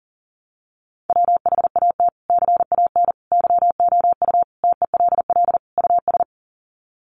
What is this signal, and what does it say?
Morse code, heard as a keyed tone, about 37 wpm, 700 Hz. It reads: WHAT CAN YOU TELL US
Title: Sets of 5 Words Encore 40wpm